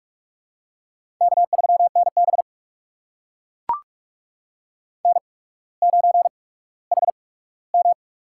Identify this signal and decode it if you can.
Morse code, heard as a keyed tone, about 44 wpm, spaced as for 19 wpm, 690 Hz. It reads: K3NB N9HM
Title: USA Call Signs - 4 Characters - 8x Character Spacing - 45wpm